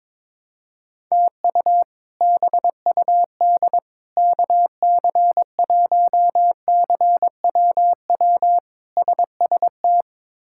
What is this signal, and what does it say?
Morse code, heard as a keyed tone, about 22 wpm, 700 Hz. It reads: TU BUD KC1CWW SST
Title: K1USN SST QSO Practice - 22wpm